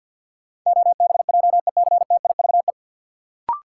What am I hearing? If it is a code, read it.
Morse code, heard as a keyed tone, about 53 wpm, 700 Hz. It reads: OBJECTIVE
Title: Sentences from Top 1200 Words - New Word Drill Alt - 50wpm